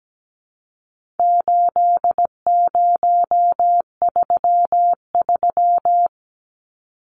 Morse code, heard as a keyed tone, about 17 wpm, 700 Hz. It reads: 8033
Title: Numbers - 4-Digit 17wpm